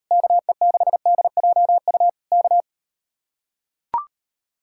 Morse code, heard as a keyed tone, about 39 wpm, 700 Hz. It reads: KE6DJU K